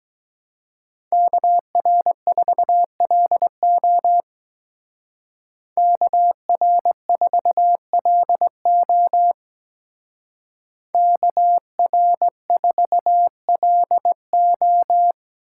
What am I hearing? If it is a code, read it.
Morse code, heard as a keyed tone, about 20 wpm, 700 Hz. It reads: KR4LO KR4LO KR4LO
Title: Speed Racing - USA 2x2 Call Signs 17wpm